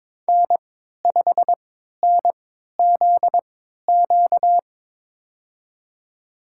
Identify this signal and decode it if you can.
Morse code, heard as a keyed tone, about 22 wpm, spaced as for 13 wpm, 700 Hz. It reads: N5NZQ
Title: USA Call Signs - 5 Characters - 3x Character Spacing - 22wpm